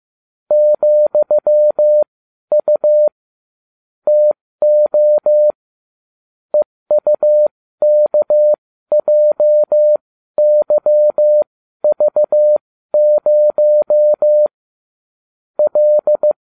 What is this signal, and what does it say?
Morse code, heard as a keyed tone, about 15 wpm, spaced as for 12 wpm, 600 Hz. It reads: ,U TO EUKJYV0 L